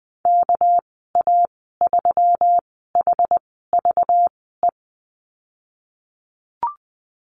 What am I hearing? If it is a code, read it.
Morse code, heard as a keyed tone, about 20 wpm, spaced as for 15 wpm, 700 Hz. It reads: KA3HVE
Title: 213 Lesson 34g - Number 8 - Call Signs 20wpm